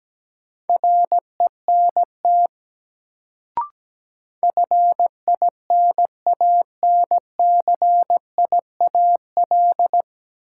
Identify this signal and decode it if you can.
Morse code, heard as a keyed tone, about 17 wpm, 700 Hz. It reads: RENT FINANCIAL
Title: Sentences from Top 1000 Words - Review Part 5 - New Word Drill - 17wpm